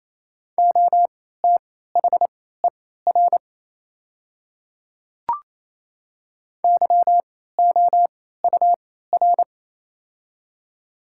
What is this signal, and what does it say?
Morse code, heard as a keyed tone, about 28 wpm, spaced as for 16 wpm, 700 Hz. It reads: OTHER YOUR
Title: 3x Character Spacing - Top 100 Words - 28wpm